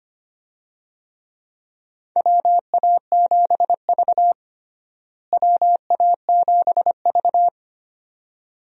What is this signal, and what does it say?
Morse code, heard as a keyed tone, about 25 wpm, 700 Hz. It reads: WA7V WA7V
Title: USA Call Signs - 4 Characters - Repeated 2x - 25wpm